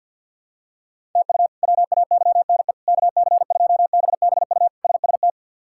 Morse code, heard as a keyed tone, about 51 wpm, 690 Hz. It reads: TU WAYNE KC2BBU SST
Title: K1USN SST QSO Practice - 50wpm